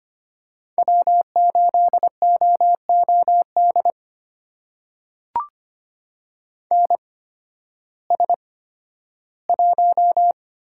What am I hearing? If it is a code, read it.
Morse code, heard as a keyed tone, about 25 wpm, spaced as for 13 wpm, 700 Hz. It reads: W8OOD NS1
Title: USA Call Signs - 5 Characters - 8x Character Spacing - 25wpm